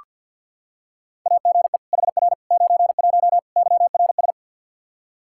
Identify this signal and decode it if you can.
Morse code, heard as a keyed tone, about 46 wpm, 700 Hz. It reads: AGE HR 91 YRS
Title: Practice QSO - Exchange 3 - More Info - 50wpm